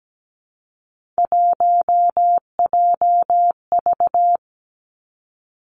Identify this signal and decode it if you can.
Morse code, heard as a keyed tone, about 17 wpm, 700 Hz. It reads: 1JV
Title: Three Challenging Letter-Number Combinations 17wpm